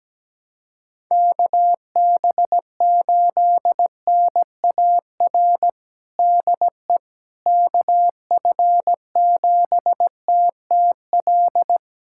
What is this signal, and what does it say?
Morse code, heard as a keyed tone, about 17 wpm, 700 Hz. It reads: KB8NAR DE KF7TTL
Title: USA Callsign DE Callsign - 6 Characters - 17wpm